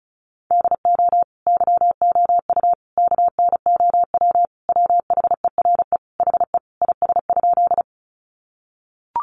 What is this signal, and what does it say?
Morse code, heard as a keyed tone, about 34 wpm, 700 Hz. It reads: DO YOU KNOW WHERE HE IS?
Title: Sentences from Top 300 Words 35wpm